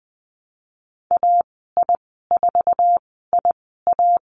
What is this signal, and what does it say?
Morse code, heard as a keyed tone, about 20 wpm, spaced as for 14 wpm, 700 Hz. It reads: AI4IA
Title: 048 Lesson 9g - Number 4 - Call Signs 20wpm